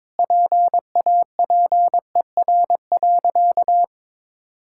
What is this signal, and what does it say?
Morse code, heard as a keyed tone, about 22 wpm, 690 Hz. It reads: PAPER.
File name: practice / 22wpm / Easy Sentences with Hints - 22wpm